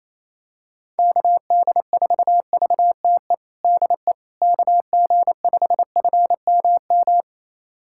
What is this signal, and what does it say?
Morse code, heard as a keyed tone, about 28 wpm, 700 Hz. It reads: KD4VTE DE KG5FMM
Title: USA Callsign DE Callsign - 6 Characters - 28wpm